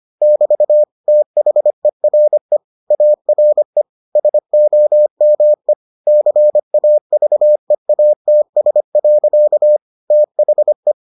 Morse code, heard as a keyed tone, about 25 wpm, 600 Hz. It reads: <BT> THERE ARE SOME CAVEATS. THE